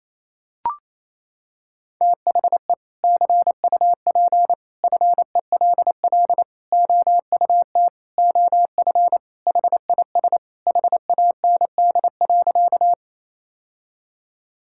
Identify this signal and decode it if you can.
Morse code, heard as a keyed tone, about 28 wpm, 700 Hz. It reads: THE CUP FELL OUT OF HIS HAND.
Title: Sentences from Top 1600 Words 28wpm